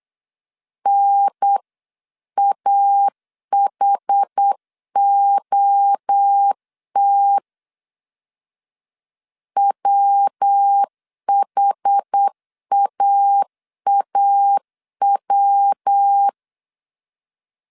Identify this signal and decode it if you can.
Morse code, heard as a keyed tone, about 8 wpm, 800 Hz. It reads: NAHOT WHAAW